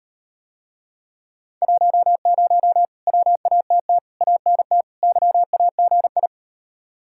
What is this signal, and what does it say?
Morse code, heard as a keyed tone, about 36 wpm, 700 Hz. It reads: 10 WATT ANT YAGI